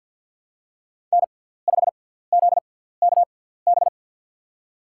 Morse code, heard as a keyed tone, about 51 wpm, spaced as for 17 wpm, 700 Hz. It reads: N5ZKB
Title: USA Call Signs - 5 Characters - 6x Character Spacing - 50wpm